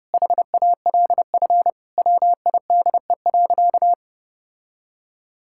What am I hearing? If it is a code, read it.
Morse code, heard as a keyed tone, about 30 wpm, 700 Hz. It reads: HALF WIDE.
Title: Sentences from Top 1000 Words - Review Part 6 - 30wpm